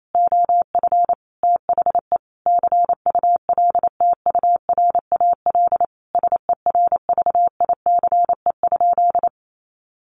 Morse code, heard as a keyed tone, about 28 wpm, 700 Hz. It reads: OF THE CULTURAL SERVICE?
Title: Sentences from Top 1500 Words 28wpm